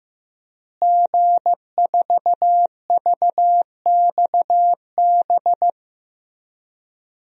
Morse code, heard as a keyed tone, about 15 wpm, 700 Hz. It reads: G4VXB
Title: DXCC-100 Call Signs - 5 Characters - Repeated 2x - 15wpm